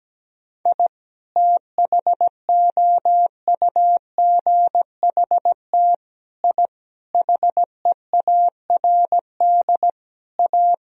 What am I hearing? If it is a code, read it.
Morse code, heard as a keyed tone, about 17 wpm, 700 Hz. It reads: I THOUGHT I HEARD A
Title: Sentences from Top 1000 Words - Review Part 2 - 17wpm